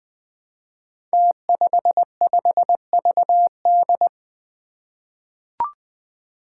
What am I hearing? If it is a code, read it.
Morse code, heard as a keyed tone, about 20 wpm, 700 Hz. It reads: T55VD
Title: Five Challenging Letter-Number Combinations 20wpm